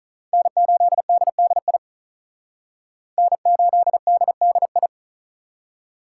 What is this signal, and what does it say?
Morse code, heard as a keyed tone, about 44 wpm, 700 Hz. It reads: N8DDI N8DDI
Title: Speed Racing - USA Call Signs 35wpm